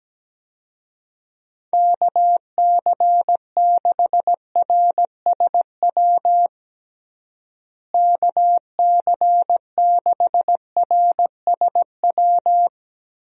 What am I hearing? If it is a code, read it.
Morse code, heard as a keyed tone, about 17 wpm, 700 Hz. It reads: KC6RSW KC6RSW
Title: USA Call Signs - 6 Characters - Repeated 3x - 17wpm